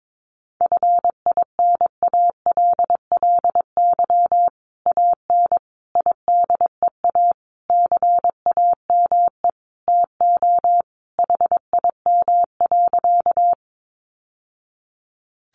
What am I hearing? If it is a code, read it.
Morse code, heard as a keyed tone, about 22 wpm, 700 Hz. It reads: FINALLY AN IDEA CAME TO HIM.
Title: Sentences from Top 1000 Words 22wpm